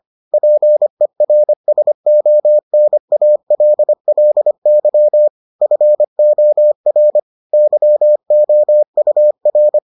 Morse code, heard as a keyed tone, about 25 wpm, 600 Hz. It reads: PERSONALLY FOR YOUR